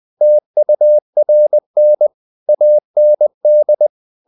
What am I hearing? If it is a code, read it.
Morse code, heard as a keyed tone, about 20 wpm, 600 Hz. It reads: TURN AND
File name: the old man and the sea / the old man and the sea part 1 20wpm